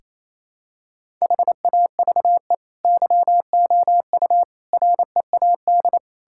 Morse code, heard as a keyed tone, about 28 wpm, 700 Hz. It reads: HAVE YOU READ
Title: Sentences from Top 1000 Words - Review Part 5 - 28wpm